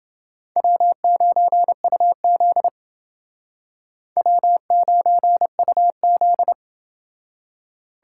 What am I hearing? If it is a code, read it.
Morse code, heard as a keyed tone, about 31 wpm, 700 Hz. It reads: W9UZ W9UZ